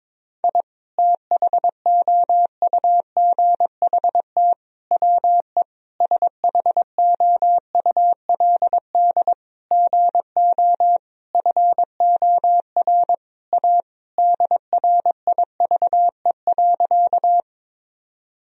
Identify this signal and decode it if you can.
Morse code, heard as a keyed tone, about 22 wpm, 700 Hz. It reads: I THOUGHT WE SHOULD GO FOR A DRIVE.
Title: Sentences from Top 1000 Words - Review Part 2 - 22wpm